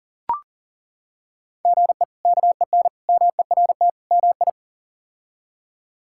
Morse code, heard as a keyed tone, about 40 wpm, 700 Hz. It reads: GE KEN MERT MI
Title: K1USN SST QSO Practice - 40wpm